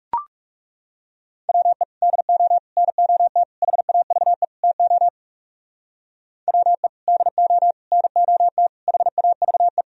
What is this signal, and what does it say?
Morse code, heard as a keyed tone, about 39 wpm, 700 Hz. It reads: WE DO NOT HAVE TO WE DO NOT HAVE